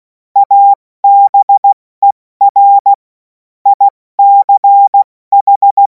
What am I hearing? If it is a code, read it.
Morse code, heard as a keyed tone, about 16 wpm, 800 Hz. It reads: ABER ICH
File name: Morsekurs / Witze / Witz 07. Juli 2026 70 F80